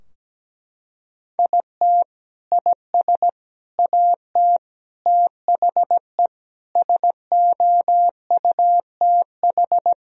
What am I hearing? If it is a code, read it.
Morse code, heard as a keyed tone, about 17 wpm, 700 Hz. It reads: IT IS AT THE SOUTH
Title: Sentences from Top 1000 Words - Review Part 2 - 17wpm